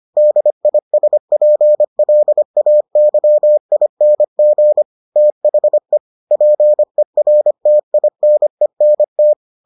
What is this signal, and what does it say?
Morse code, heard as a keyed tone, about 25 wpm, 600 Hz. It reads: DISPLAYING THE PERTINENT